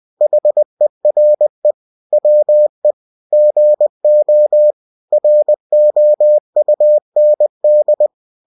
Morse code, heard as a keyed tone, about 20 wpm, 600 Hz. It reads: HERE WE GO ROUND
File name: cw-books / winnie-the-pooh / winnie-20wpm-page-0045